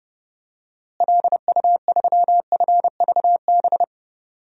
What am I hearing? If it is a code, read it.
Morse code, heard as a keyed tone, about 30 wpm, 700 Hz. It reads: LU3FVB
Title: DXCC-100 Call Signs - 6 Characters - Repeated 3x - 30wpm